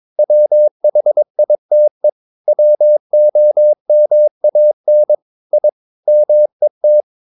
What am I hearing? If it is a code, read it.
Morse code, heard as a keyed tone, about 22 wpm, 600 Hz. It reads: WHITE WOMAN I MET
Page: Romanen Never af Ken Follet på engelsk.